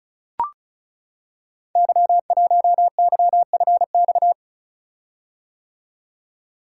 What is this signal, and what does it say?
Morse code, heard as a keyed tone, about 36 wpm, 690 Hz. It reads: Y1YFX
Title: Code Groups - 5 Letter:Numbers - 35wpm